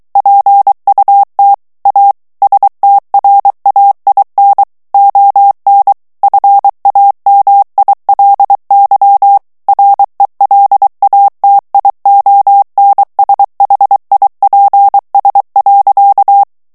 Morse code, heard as a keyed tone, about 23 wpm, 800 Hz. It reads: PUT A STRAIN ON FAMILY RELATIONSHIPS.
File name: morse mp3 / ANGLAIS texte clair 1 vitesse 22 mots minute